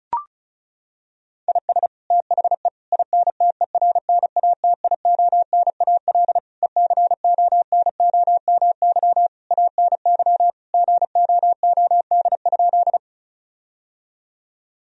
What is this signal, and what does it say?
Morse code, heard as a keyed tone, about 36 wpm, 700 Hz. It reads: IS THE INTERNATIONAL ECONOMY ANY GOOD?
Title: Sentences from Top 1000 Words - Review Part 4 - 35wpm